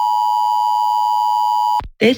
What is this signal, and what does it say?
Emergency Alert System (SAME) tones (the 853 and 960 Hz attention tone), with speech. It is heard from 0.0 to 1.9 s.